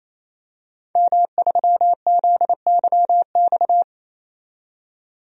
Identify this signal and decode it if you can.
Morse code, heard as a keyed tone, about 28 wpm, 700 Hz. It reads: M3ZYX
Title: DXCC-100 Call Signs - 5 Characters - 28wpm